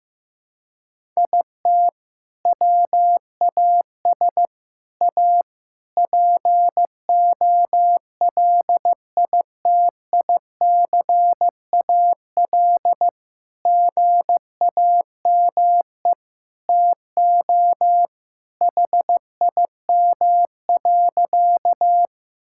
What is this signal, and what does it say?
Morse code, heard as a keyed tone, about 15 wpm, 690 Hz. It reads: IT WAS A POLITICAL GAME TO HIM.
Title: Sentences from Top 1000 Words - Review Part 5 - 15wpm